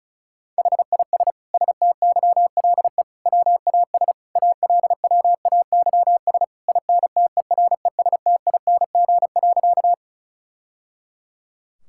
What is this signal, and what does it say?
Morse code, heard as a keyed tone, about 34 wpm, 700 Hz. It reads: HIS STYLE WAS ALWAYS INTERESTING.
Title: Sentences from Top 1000 Words - Review Part 5 - 35wpm